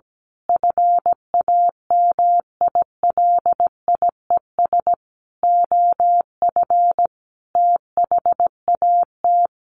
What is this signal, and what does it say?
Morse code, heard as a keyed tone, about 17 wpm, 700 Hz. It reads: FAMILIES OF THAT
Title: A Sound Of Thunder 17wpm